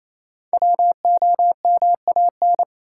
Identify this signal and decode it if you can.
Morse code, heard as a keyed tone, about 28 wpm, 700 Hz. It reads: WOMAN
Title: Sentences from Top 1000 Words - Review Part 2 - New Word Drill - 28wpm